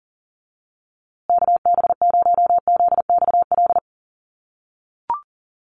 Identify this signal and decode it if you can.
Morse code, heard as a keyed tone, about 40 wpm, 700 Hz. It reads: KB0ZXL
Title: USA Call Signs Repeated 2x 40wpm